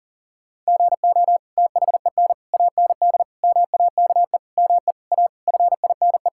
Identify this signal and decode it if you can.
Morse code, heard as a keyed tone, about 40 wpm, 700 Hz. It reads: GO THEN AND MAKE ME A FINE
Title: Sentences from Top 1000 Words - Review Part 5 - 40wpm